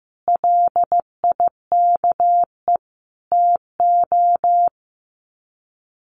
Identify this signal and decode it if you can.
Morse code, heard as a keyed tone, about 15 wpm, 700 Hz. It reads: LIKE TO
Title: Sets of 4 Words 15wpm